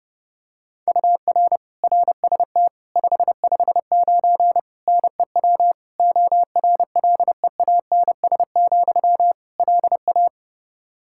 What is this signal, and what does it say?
Morse code, heard as a keyed tone, about 31 wpm, 700 Hz. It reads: UR RST 559 NEW ORLEANS, LA
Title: Practice QSO - Exchange 1 - Beginning - 30wpm